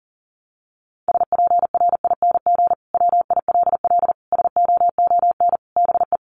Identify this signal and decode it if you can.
Morse code, heard as a keyed tone, about 40 wpm, 700 Hz. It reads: SPRING WILL SOON BE
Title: Sentences from Top 1200 Words 40wpm